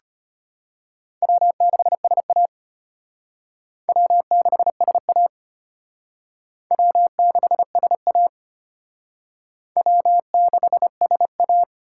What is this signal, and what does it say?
Morse code, heard as a keyed tone, about 32 wpm, 700 Hz. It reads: W6SA W6SA W6SA W6SA